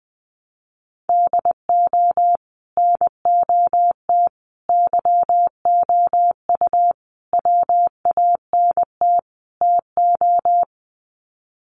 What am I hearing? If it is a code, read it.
Morse code, heard as a keyed tone, about 20 wpm, 700 Hz. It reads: DO NOT YOU WANT TO
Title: Sets of 5 Words Encore2 20wpm